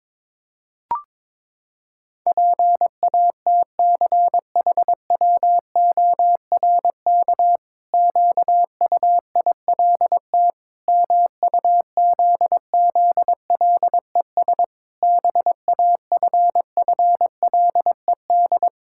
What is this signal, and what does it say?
Morse code, heard as a keyed tone, about 22 wpm, 700 Hz. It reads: PATCHWORK QUILT MUZZLES BAFFLED
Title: Pangram Challenge v2 - 22wpm